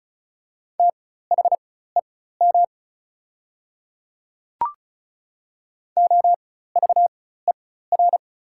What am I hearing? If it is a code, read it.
Morse code, heard as a keyed tone, about 34 wpm, spaced as for 16 wpm, 690 Hz. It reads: THEM OVER
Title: 4x Character Spacing - Top 100 Words - 35wpm